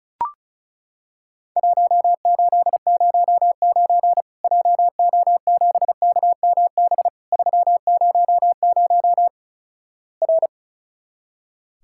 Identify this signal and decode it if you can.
Morse code, heard as a keyed tone, about 36 wpm, 700 Hz. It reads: 1809 JO7KMB 300 R
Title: Worked All Europe DX Contest - QTC Exchange - 35wpm